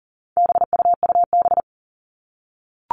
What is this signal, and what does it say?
Morse code, heard as a keyed tone, about 40 wpm, 700 Hz. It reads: BUUB
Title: All Cut Numbers - 4-Digit 40wpm